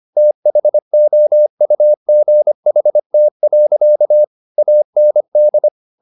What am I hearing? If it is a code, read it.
Morse code, heard as a keyed tone, about 25 wpm, 600 Hz. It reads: THOUGHT. AND